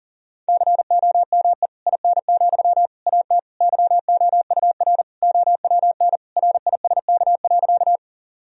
Morse code, heard as a keyed tone, about 40 wpm, 700 Hz. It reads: COME IN, AT YOUR OWN RISK.
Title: Sentences from Top 1000 Words - Review Part 4 - 40wpm